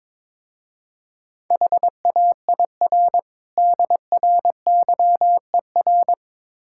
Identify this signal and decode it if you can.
Morse code, heard as a keyed tone, about 22 wpm, 700 Hz. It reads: HAIR DRYER
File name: Easy Words - Speed-Racing with Hints - 22wpm